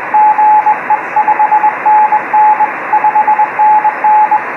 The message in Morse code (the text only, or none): GE 5NN 5NN